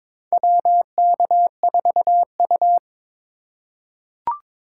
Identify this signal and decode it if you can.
Morse code, heard as a keyed tone, about 22 wpm, 700 Hz. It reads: WK4U
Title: USA Call Signs - 4 Characters - 3x Character Spacing - 22wpm